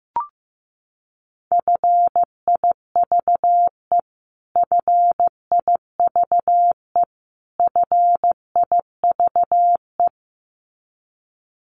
Morse code, heard as a keyed tone, about 15 wpm, 700 Hz. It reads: FIVE FIVE FIVE
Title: Top 500 Words Repeated 15wpm